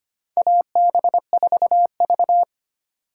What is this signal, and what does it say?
Morse code, heard as a keyed tone, about 25 wpm, 700 Hz. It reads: AB4V
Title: USA Call Signs - 2x Character Spacing - 25wpm